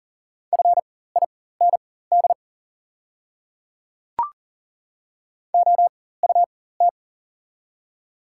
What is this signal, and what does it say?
Morse code, heard as a keyed tone, about 39 wpm, spaced as for 19 wpm, 700 Hz. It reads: FIND OUT
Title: 4x Character Spacing - Top 100 Words - 40wpm